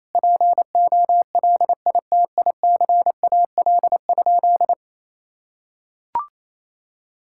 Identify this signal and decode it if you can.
Morse code, heard as a keyed tone, about 28 wpm, 700 Hz. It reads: POLITICAL?
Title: Sentences from Top 1000 Words - Review Part 5 - 28wpm